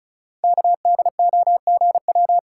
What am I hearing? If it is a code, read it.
Morse code, heard as a keyed tone, about 36 wpm, 690 Hz. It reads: KDOGW
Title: Code Groups - 5 Letters - 35wpm